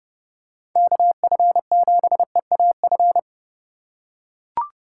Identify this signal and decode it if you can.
Morse code, heard as a keyed tone, about 32 wpm, 700 Hz. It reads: KF7EAF
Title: USA Call Signs - 6 Characters - 3x Character Spacing - 30wpm